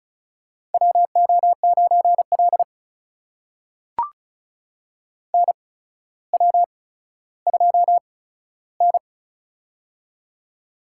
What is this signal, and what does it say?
Morse code, heard as a keyed tone, about 36 wpm, spaced as for 15 wpm, 700 Hz. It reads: WO9L NW2N
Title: USA Call Signs - 4 Characters - 8x Character Spacing - 35wpm